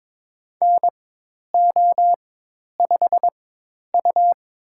Morse code, heard as a keyed tone, about 23 wpm, spaced as for 10 wpm, 700 Hz. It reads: NO5U
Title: USA Call Signs - 4 Characters - 4x Character Spacing - 22wpm